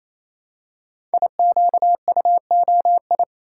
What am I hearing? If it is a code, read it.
Morse code, heard as a keyed tone, about 27 wpm, 690 Hz. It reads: IQUOI
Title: Code Groups - 5 Letters - 28wpm